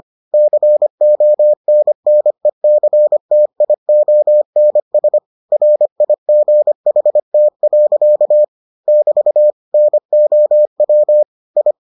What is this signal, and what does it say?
Morse code, heard as a keyed tone, about 25 wpm, 600 Hz. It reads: CONNECTIONS RIGHT. <BT> NOW I